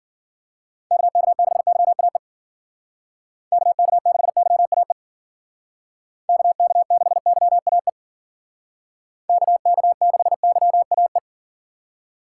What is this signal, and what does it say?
Morse code, heard as a keyed tone, about 50 wpm, 700 Hz. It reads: KK6YAE KK6YAE KK6YAE KK6YAE
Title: Speed Racing - USA 2x3 Call Signs 40wpm